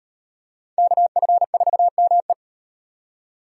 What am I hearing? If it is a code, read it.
Morse code, heard as a keyed tone, about 38 wpm, 700 Hz. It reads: KF4ME